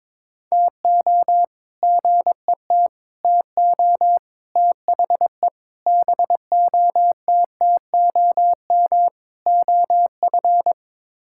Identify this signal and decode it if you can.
Morse code, heard as a keyed tone, about 22 wpm, 700 Hz. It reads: TO GET TO THE BOTTOM OF
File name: practice / 22wpm / Sets of 6 Words - 22wpm